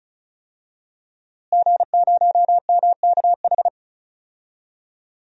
Morse code, heard as a keyed tone, about 36 wpm, 700 Hz. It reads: G0MKH
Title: DXCC-100 Call Signs - 5 Characters - Repeated 2x - 35wpm